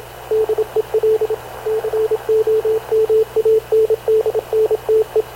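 Morse code, heard as a keyed tone, about 28 wpm, 440 Hz, with radio noise. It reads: DEL COMANDNTE
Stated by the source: Havana Radio / CLA sending Spanish press (PX) in Morse, recorded by Radio Officer Paul Dumesnil onboard CNAV QUEST/CZDO.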